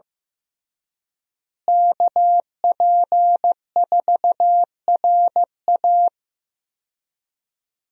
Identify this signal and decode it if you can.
Morse code, heard as a keyed tone, about 15 wpm, 700 Hz. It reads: KP4RA
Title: DXCC-100 Call Signs - 5 Characters - Repeated 3x - 15wpm